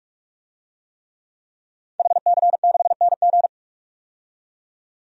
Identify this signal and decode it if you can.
Morse code, heard as a keyed tone, about 46 wpm, 700 Hz. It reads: HC6NG
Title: DXCC-100 Call Signs - 5 Characters - 45wpm